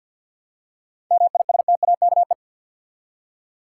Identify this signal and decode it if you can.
Morse code, heard as a keyed tone, about 47 wpm, 700 Hz. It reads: MISTAKE
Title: Seven Letter Words - 50wpm